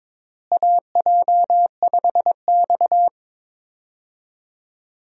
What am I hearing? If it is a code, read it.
Morse code, heard as a keyed tone, about 22 wpm, 700 Hz. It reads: AJ5X
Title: USA Call Signs - 4 Characters - Repeated 2x - 22wpm